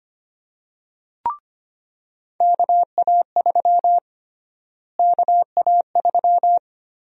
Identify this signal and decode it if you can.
Morse code, heard as a keyed tone, about 25 wpm, 700 Hz. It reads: KA3 KA3
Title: USA Call Sign Prefixes - Repeated 2x - 25wpm